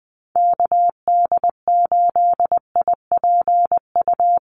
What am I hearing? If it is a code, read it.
Morse code, heard as a keyed tone, about 20 wpm, 700 Hz. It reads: KD8IPU
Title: USA Call Signs Repeated 2x 20wpm